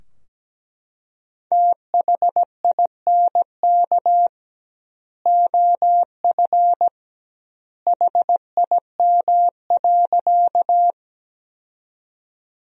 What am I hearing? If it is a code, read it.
Morse code, heard as a keyed tone, about 17 wpm, 700 Hz. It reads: THINK OF HIM.
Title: 2x Word-Spacing - Sentences from Top 200 Words - 17wpm